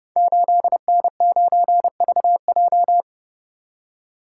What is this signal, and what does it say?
Morse code, heard as a keyed tone, about 30 wpm, 700 Hz. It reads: 8N9VJ
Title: Code Groups - 5 Letter:Numbers - 30wpm